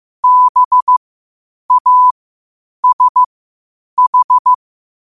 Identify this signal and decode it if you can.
Morse code, heard as a keyed tone, about 15 wpm, spaced as for 8 wpm, 1000 Hz. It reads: BASH
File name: qotdfiles / 8wpm